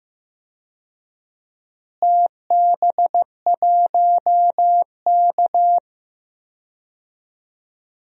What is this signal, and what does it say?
Morse code, heard as a keyed tone, about 15 wpm, 700 Hz. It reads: TB1K
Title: DXCC-100 Call Signs - 4 Characters - Repeated 2x - 15wpm